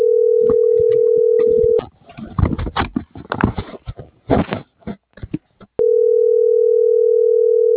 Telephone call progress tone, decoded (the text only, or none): ringback tone